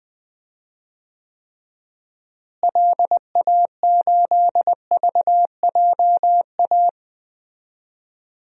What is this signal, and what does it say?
Morse code, heard as a keyed tone, about 21 wpm, 700 Hz. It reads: LA8VJA
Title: DXCC-100 Call Signs - 6 Characters - Repeated 3x - 20wpm